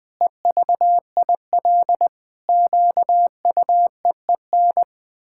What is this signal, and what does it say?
Morse code, heard as a keyed tone, about 20 wpm, 700 Hz. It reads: EVIL QUEEN
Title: Pangram Challenge v2 - 20wpm